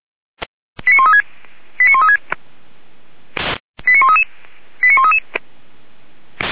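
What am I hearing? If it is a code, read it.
Five-tone selective call (ZVEI), heard as quick runs of five tones, tones of about 70 ms. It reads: 89136 89136 89133 89133
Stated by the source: Five-tone ZVEI selective call 89132 received on an SDR, from the feuerwehr-orlamuende.org audio server.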